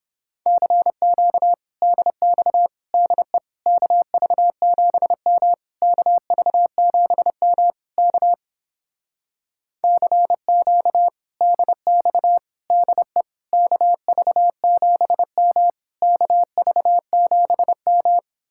Morse code, heard as a keyed tone, about 29 wpm, 700 Hz. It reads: CQ DX DE KV7M KV7M K CQ DX DE KV7M KV7M